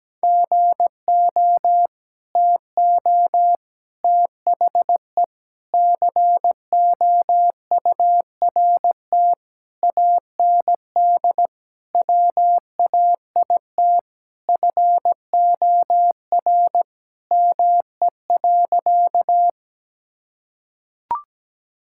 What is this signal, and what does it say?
Morse code, heard as a keyed tone, about 17 wpm, 700 Hz. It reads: GO TO THE COURT AND WAIT FOR ME.
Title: Sentences from Top 1600 Words 17wpm